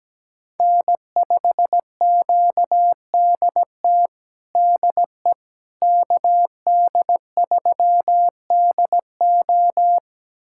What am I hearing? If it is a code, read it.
Morse code, heard as a keyed tone, about 17 wpm, 700 Hz. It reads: N5QDT DE KD3DO
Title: USA Callsign DE Callsign - 5 Characters - 17wpm